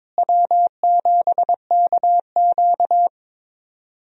Morse code, heard as a keyed tone, about 22 wpm, 700 Hz. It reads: W7KQ
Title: USA Call Signs - 4 Characters - Repeated 3x - 22wpm